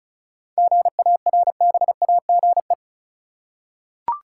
Morse code, heard as a keyed tone, about 34 wpm, 700 Hz. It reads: GARBAGE
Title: Seven Letter Words - 35wpm